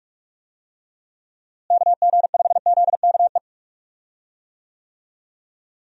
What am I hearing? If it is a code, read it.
Morse code, heard as a keyed tone, about 46 wpm, 700 Hz. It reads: KG5ZKE
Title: USA Call Signs - 6 Characters - Repeated 2x - 45wpm